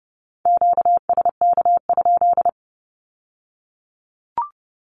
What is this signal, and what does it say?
Morse code, heard as a keyed tone, about 30 wpm, 700 Hz. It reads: QSK?
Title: Q-Codes with Spoken Meaning - 30wpm